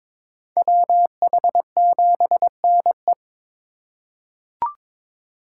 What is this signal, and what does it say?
Morse code, heard as a keyed tone, about 22 wpm, 700 Hz. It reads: WH7NE
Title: USA Call Signs - 5 Characters - 2x Character Spacing - 22wpm